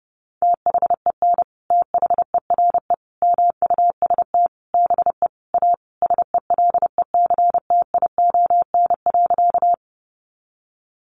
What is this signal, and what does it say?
Morse code, heard as a keyed tone, about 30 wpm, 700 Hz. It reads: THEN THERE MUST BE A SELECTION.
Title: Sentences from Top 1400 Words 30wpm